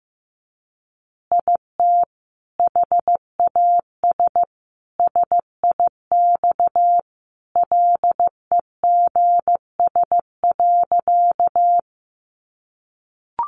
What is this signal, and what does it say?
Morse code, heard as a keyed tone, about 15 wpm, 700 Hz. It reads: IT HAS SIX LEGS.
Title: Additional Easy Phrases 15wpm